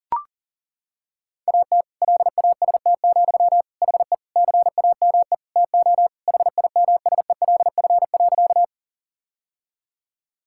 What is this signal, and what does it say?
Morse code, heard as a keyed tone, about 40 wpm, 700 Hz. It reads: AT LAST, HE CAME TO HIMSELF.
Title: Sentences from Top 1600 Words 40wpm